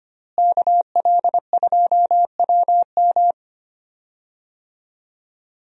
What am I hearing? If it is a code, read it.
Morse code, heard as a keyed tone, about 25 wpm, 700 Hz. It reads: KL2WM
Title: USA Call Signs - 5 Characters - Repeated 3x - 25wpm